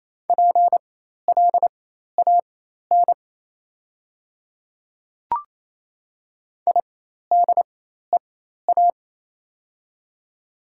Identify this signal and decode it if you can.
Morse code, heard as a keyed tone, about 28 wpm, spaced as for 13 wpm, 690 Hz. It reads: PLAN IDEA
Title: 4x Character Spacing - Top 500 Words - 28wpm